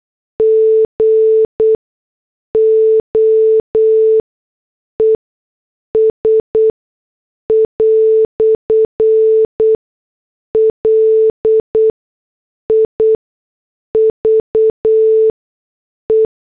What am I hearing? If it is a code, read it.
Morse code, heard as a keyed tone, about 8 wpm, spaced as for 6 wpm, 440 Hz. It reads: GOES "LIVE
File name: morse / headlines